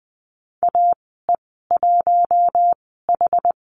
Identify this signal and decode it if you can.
Morse code, heard as a keyed tone, about 20 wpm, spaced as for 14 wpm, 700 Hz. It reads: AE1H
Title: 062 Lesson 11g - Letter h - Call Signs 20wpm